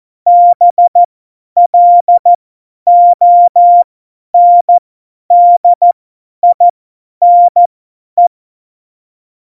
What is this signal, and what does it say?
Morse code, heard as a keyed tone, about 14 wpm, spaced as for 10 wpm, 700 Hz. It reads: BLONDINE
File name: Morsekurs / Witze / Witz 11 Juli 17, Tempo 50 F70